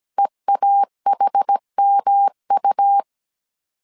Morse code, heard as a keyed tone, about 17 wpm, 800 Hz. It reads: EAHMU